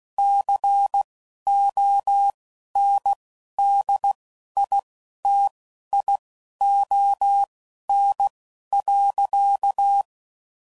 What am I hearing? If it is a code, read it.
Morse code, heard as a keyed tone, about 16 wpm, spaced as for 11 wpm, 800 Hz. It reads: CONDITION.
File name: CW Code Groups @10WPM/Technical Subjects